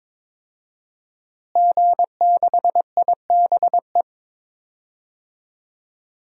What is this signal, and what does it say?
Morse code, heard as a keyed tone, about 22 wpm, 700 Hz. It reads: G6IBE
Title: DXCC-100 Call Signs - 5 Characters - Repeated 2x - 22wpm